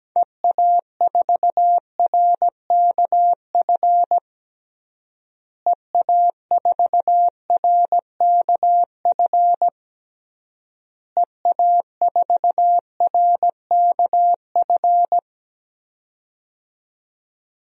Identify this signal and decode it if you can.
Morse code, heard as a keyed tone, about 17 wpm, 700 Hz. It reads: EA4RKF EA4RKF EA4RKF
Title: DXCC-100 Call Signs - 6 Characters - Repeated 3x - 17wpm